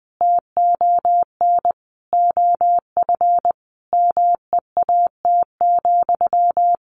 Morse code, heard as a keyed tone, about 20 wpm, 700 Hz. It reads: TON OF MEAT,
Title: A Sound Of Thunder 20wpm